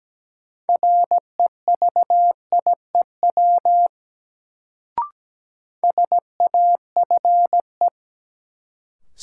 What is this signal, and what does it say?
Morse code, heard as a keyed tone, about 17 wpm, 700 Hz. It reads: REVIEW SAFE
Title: Sentences from Top 900 Words - New Word Drill - 17wpm